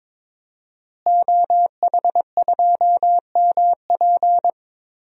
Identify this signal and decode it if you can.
Morse code, heard as a keyed tone, about 22 wpm, 700 Hz. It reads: OH2MP
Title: DXCC-100 Call Signs - 5 Characters - Repeated 2x - 22wpm